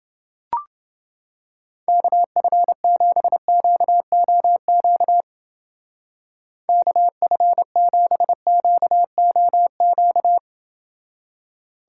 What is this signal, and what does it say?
Morse code, heard as a keyed tone, about 28 wpm, 700 Hz. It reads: KF7QOQ KF7QOQ